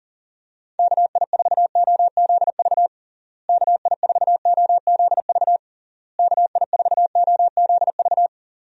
Morse code, heard as a keyed tone, about 37 wpm, 700 Hz. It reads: KI4OZV KI4OZV KI4OZV
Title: USA Call Signs - 6 Characters - Repeated 3x - 40wpm